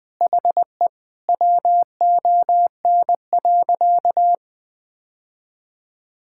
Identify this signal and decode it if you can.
Morse code, heard as a keyed tone, about 20 wpm, 700 Hz. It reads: HE WON.
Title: Sentences with 7-Word Phrases - 20wpm